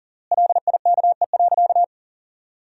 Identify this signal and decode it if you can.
Morse code, heard as a keyed tone, about 37 wpm, 700 Hz. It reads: LIKE.
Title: 3x Word-Spacing - Sentences from Top 500 Words - 40wpm